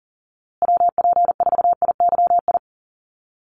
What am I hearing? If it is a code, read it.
Morse code, heard as a keyed tone, about 40 wpm, 700 Hz. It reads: WP4IYI
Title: USA Call Signs Repeated 3x 40wpm